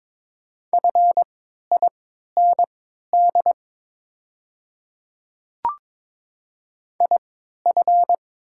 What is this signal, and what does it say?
Morse code, heard as a keyed tone, about 22 wpm, spaced as for 12 wpm, 700 Hz. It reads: FIND IF
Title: 3x Character Spacing - Top 100 Words - 22wpm